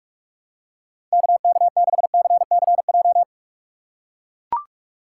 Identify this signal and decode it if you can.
Morse code, heard as a keyed tone, about 44 wpm, 700 Hz. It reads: KK6CCJ
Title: USA Call Signs - 4x Character Spacing - 45wpm